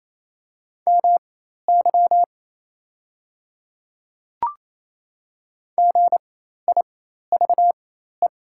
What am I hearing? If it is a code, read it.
Morse code, heard as a keyed tone, about 28 wpm, spaced as for 13 wpm, 700 Hz. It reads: MY GIVE
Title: 4x Character Spacing - Top 500 Words - 28wpm